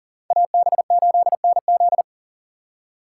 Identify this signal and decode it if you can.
Morse code, heard as a keyed tone, about 40 wpm, 700 Hz. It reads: AB8NZ
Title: USA Call Signs - 5 Characters - 3x Character Spacing - 40wpm